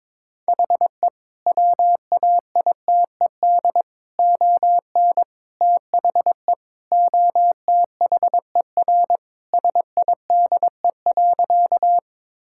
Morse code, heard as a keyed tone, about 22 wpm, 700 Hz. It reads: HE WAITED ON THE OTHER SIDE.
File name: practice / 22wpm / Sentences with 4-Word Phrases - 22wpm